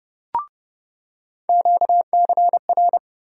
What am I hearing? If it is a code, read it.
Morse code, heard as a keyed tone, about 30 wpm, 700 Hz. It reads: QCR
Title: USA Call Sign Postfixes - 3 Letters 30wpm